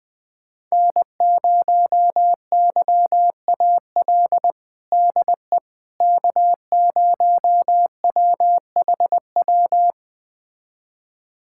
Morse code, heard as a keyed tone, about 20 wpm, 700 Hz. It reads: N0YAL DE K0WHW
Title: USA Callsign DE Callsign - 5 Characters - 20wpm